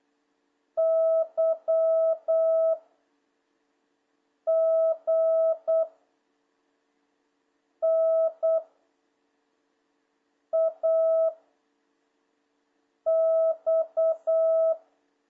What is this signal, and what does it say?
Morse code, heard as a keyed tone, about 8 wpm, 660 Hz. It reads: Y G N A X
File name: jh8cbh / 2026cw